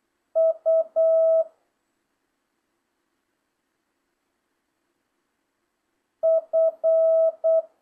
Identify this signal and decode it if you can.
Morse code, heard as a keyed tone, about 8 wpm, 650 Hz. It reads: U F